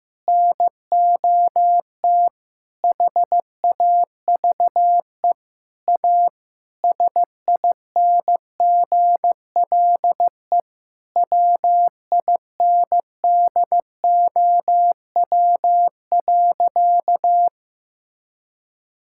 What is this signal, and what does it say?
Morse code, heard as a keyed tone, about 15 wpm, 700 Hz. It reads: NOT HAVE A SINGLE WINDOW.
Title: Sentences from Top 1600 Words 15wpm